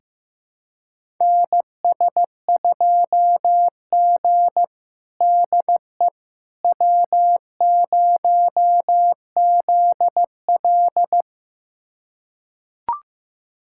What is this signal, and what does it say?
Morse code, heard as a keyed tone, about 15 wpm, 700 Hz. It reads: NS2G DE W0ZL
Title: USA Callsign DE Callsign - 4 Characters - 15wpm